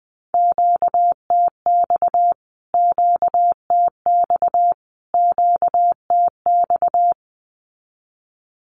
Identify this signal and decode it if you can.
Morse code, heard as a keyed tone, about 20 wpm, 700 Hz. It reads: QTX QTX QTX
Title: Q-Codes Repeated 3x 20wpm